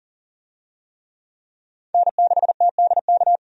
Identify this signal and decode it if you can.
Morse code, heard as a keyed tone, about 40 wpm, 700 Hz. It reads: N6TDK
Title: USA Call Signs - 5 Characters - 40wpm